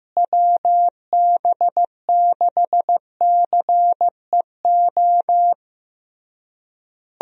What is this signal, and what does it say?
Morse code, heard as a keyed tone, about 15 wpm, 700 Hz. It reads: WB6CEO
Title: USA Call Signs - 6 Characters - Repeated 3x - 15wpm